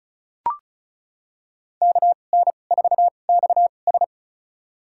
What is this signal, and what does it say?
Morse code, heard as a keyed tone, about 36 wpm, spaced as for 25 wpm, 700 Hz. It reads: KN4XS
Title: USA Call Signs - 2x Character Spacing - 35wpm